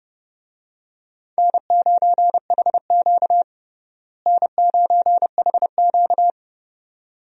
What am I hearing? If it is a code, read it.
Morse code, heard as a keyed tone, about 30 wpm, 700 Hz. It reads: N9HQ N9HQ
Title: USA Call Signs - 4 Characters - Repeated 2x - 30wpm